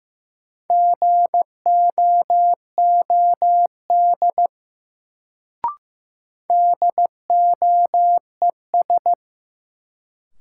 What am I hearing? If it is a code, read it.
Morse code, heard as a keyed tone, about 15 wpm, 700 Hz. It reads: GOOD DOES
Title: Sets of 2 Words - New Word Drill - 15wpm